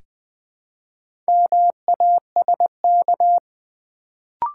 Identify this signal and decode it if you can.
Morse code, heard as a keyed tone, about 20 wpm, 700 Hz. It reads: MASK
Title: Build A Word - 2 Letters At A Time - 4-Letter Words - 20wpm